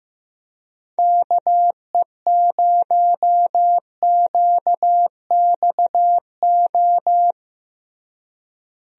Morse code, heard as a keyed tone, about 15 wpm, 700 Hz. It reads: KE0QXO
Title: USA Call Signs - 6 Characters - Repeated 3x - 15wpm